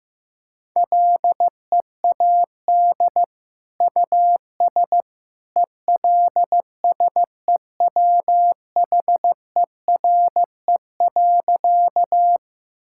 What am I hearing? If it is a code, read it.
Morse code, heard as a keyed tone, about 15 wpm, 700 Hz. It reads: LEAD US ELSEWHERE.
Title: Sentences from Top 1600 Words 15wpm